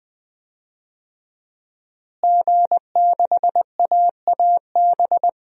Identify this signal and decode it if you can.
Morse code, heard as a keyed tone, about 20 wpm, 700 Hz. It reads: G6AAB
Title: DXCC-100 Call Signs - 5 Characters - Repeated 2x - 20wpm